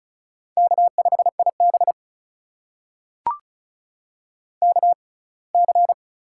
Morse code, heard as a keyed tone, about 36 wpm, spaced as for 24 wpm, 700 Hz. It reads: K5IB KC
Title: USA Call Signs - 4 Characters - 6x Character Spacing - 35wpm